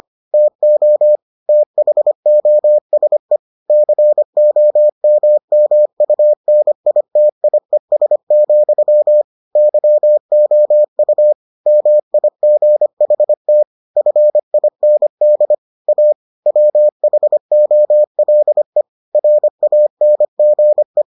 Morse code, heard as a keyed tone, about 25 wpm, 600 Hz. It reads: TO THOSE COMMUNITIES, YOU MIGHT FIND A WHOLE RANGE